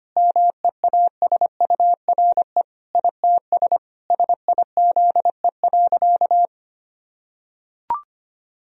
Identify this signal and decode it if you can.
Morse code, heard as a keyed tone, about 25 wpm, 700 Hz. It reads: MEASURE ITS SIZE.
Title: Sentences from Top 1000 Words - Review Part 2 - 25wpm